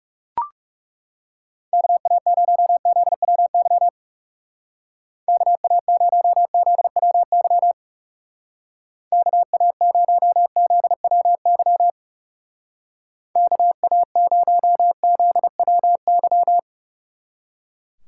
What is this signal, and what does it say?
Morse code, heard as a keyed tone, about 36 wpm, 700 Hz. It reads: KA0ZWY KA0ZWY KA0ZWY KA0ZWY
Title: Speed Racing - USA Call Signs 30wpm